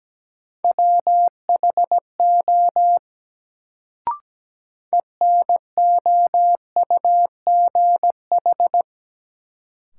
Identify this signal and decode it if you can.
Morse code, heard as a keyed tone, about 17 wpm, 700 Hz. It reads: WHO ENOUGH
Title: Sets of 2 Words - New Word Drill - 17wpm